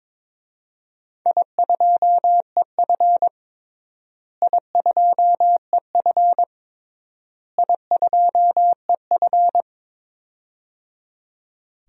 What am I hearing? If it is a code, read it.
Morse code, heard as a keyed tone, about 22 wpm, 700 Hz. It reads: I2EF I2EF I2EF
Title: DXCC-100 Call Signs - 4 Characters - Repeated 3x - 22wpm